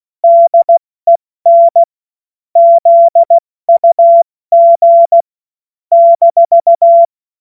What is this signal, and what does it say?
Morse code, heard as a keyed tone, about 16 wpm, spaced as for 14 wpm, 680 Hz. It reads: DEN ZUG -
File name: Morsekurs / Witze / Witz 23. Mai 2023 70 F80